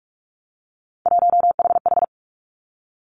Morse code, heard as a keyed tone, about 46 wpm, 700 Hz. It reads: 1HH
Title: Three Challenging Letter-Number Combinations 45wpm